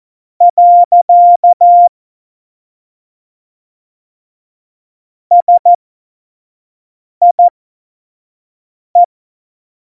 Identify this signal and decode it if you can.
Morse code, heard as a keyed tone, about 14 wpm, spaced as for 5 wpm, 700 Hz. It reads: . SIE
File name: Morsekurs / Witze / Witz 15. Mai 2018, 25 F70